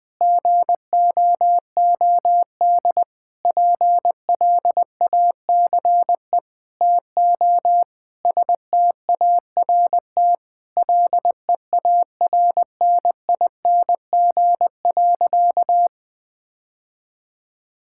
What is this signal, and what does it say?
Morse code, heard as a keyed tone, about 20 wpm, 700 Hz. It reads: GOOD PLACE TO START LEARNING.
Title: Sentences with 6-Word Phrases - 20wpm